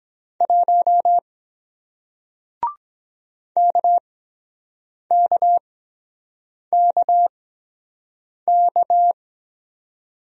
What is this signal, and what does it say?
Morse code, heard as a keyed tone, about 23 wpm, spaced as for 7 wpm, 700 Hz. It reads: 1 KKKK